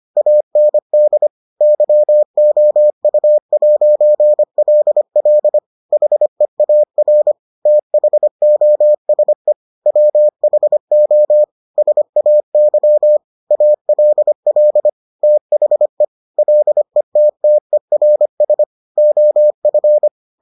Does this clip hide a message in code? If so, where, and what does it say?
Yes: Morse code from 0.2 to 20.1 s, heard as a keyed tone, about 25 wpm, 600 Hz. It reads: AND YOU'LL HEAR THOSE WHO SAY ALL THE LETTERS OF